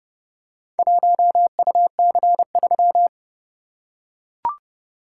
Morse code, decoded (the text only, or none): 1UC3